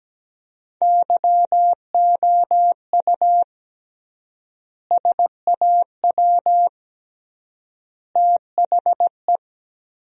Morse code, heard as a keyed tone, about 17 wpm, 700 Hz. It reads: YOU SAW THE
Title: 3x Word-Spacing - Sentences from Top 400 Words - 17wpm